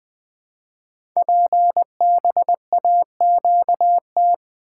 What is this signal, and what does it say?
Morse code, heard as a keyed tone, about 20 wpm, 700 Hz. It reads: PBAQT
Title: Code Groups - 5 Letters - 20wpm